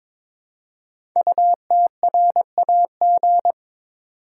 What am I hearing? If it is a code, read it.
Morse code, heard as a keyed tone, about 23 wpm, 700 Hz. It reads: UTRAG
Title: Code Groups - 5 Letter:Numbers - 22wpm